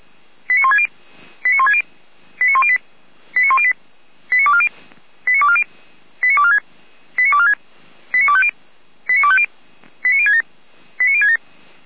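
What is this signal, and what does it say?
Five-tone selective call (ZVEI), heard as quick runs of five tones, tones of about 70 ms. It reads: 89160 89160 89108 89108 89240 89240 89246 89246 89250 89250 89067 89067
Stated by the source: Five-tone ZVEI selective call 89336 received on SDR.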